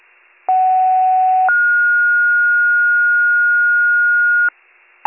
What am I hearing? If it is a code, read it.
Two-tone page, heard as a one-second tone, then a three-second tone, about 750 then 1430 Hz. It is heard from 0.5 to 4.5 s.